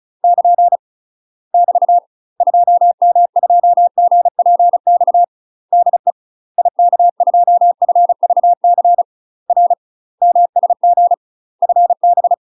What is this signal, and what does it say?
Morse code, heard as a keyed tone, about 36 wpm, 700 Hz. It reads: <KN> <BT> 2M2GPX DE IK2FVC R MSG FB
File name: UK MASTER QSOs TRAINING / 35WPM / UK MASTER QSOs 0009